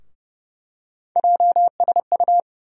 Morse code, heard as a keyed tone, about 30 wpm, 700 Hz. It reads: JSU
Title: USA Call Sign Postfixes - 3 Letters 30wpm